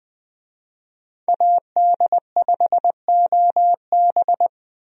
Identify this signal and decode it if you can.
Morse code, heard as a keyed tone, about 20 wpm, 700 Hz. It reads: AD5OB
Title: USA Call Signs - 5 Characters - Repeated 2x - 20wpm